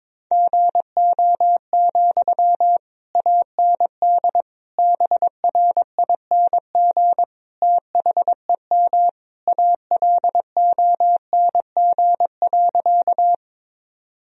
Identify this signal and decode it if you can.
Morse code, heard as a keyed tone, about 22 wpm, 700 Hz. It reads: GO, AND BRING THEM ALONG.
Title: Sentences from Top 1000 Words - Review Part 3 - 22wpm